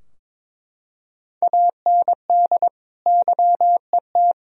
Morse code, heard as a keyed tone, about 22 wpm, 700 Hz. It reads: AND YET
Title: Sentences from Top 1000 Words - Review Part 3 - 22wpm